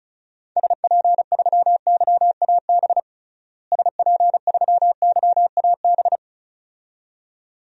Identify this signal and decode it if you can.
Morse code, heard as a keyed tone, about 35 wpm, 700 Hz. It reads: SP3YAB SP3YAB
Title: DXCC-100 Call Signs - 6 Characters - Repeated 2x - 35wpm